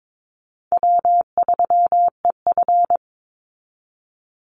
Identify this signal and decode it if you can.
Morse code, heard as a keyed tone, about 22 wpm, 700 Hz. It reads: W3EF
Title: Call Signs Repeated 3x - 22wpm